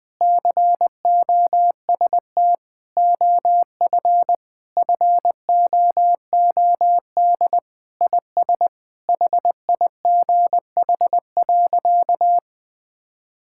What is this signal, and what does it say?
Morse code, heard as a keyed tone, about 20 wpm, 700 Hz. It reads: COST OF FOOD IS HIGH.
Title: Sentences with 3-Word Phrases - 20wpm